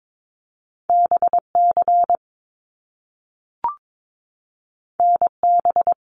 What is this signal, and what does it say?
Morse code, heard as a keyed tone, about 22 wpm, 700 Hz. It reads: BC NB
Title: Canadian Provinces and Territories 22wpm